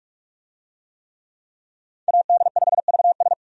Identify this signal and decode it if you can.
Morse code, heard as a keyed tone, about 44 wpm, 700 Hz. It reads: AD5VS
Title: USA Call Signs - 5 Characters - 45wpm